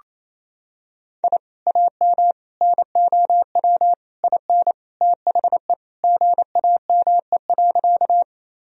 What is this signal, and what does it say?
Morse code, heard as a keyed tone, about 28 wpm, 700 Hz. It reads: I AM NOW IN THE GAME.
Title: Sentences from Top 1000 Words - Review Part 3 - 28wpm